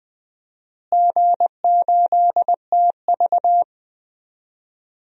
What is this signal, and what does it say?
Morse code, heard as a keyed tone, about 20 wpm, 700 Hz. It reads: G8TV
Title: DXCC-100 Call Signs - 4 Characters - Repeated 3x - 20wpm